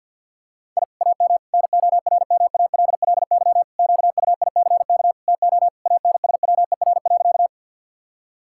Morse code, heard as a keyed tone, about 52 wpm, 700 Hz. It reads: I AM NORMALLY QUICK TO ANSWER.
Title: Sentences from Top 1000 Words - Review Part 6 - 50wpm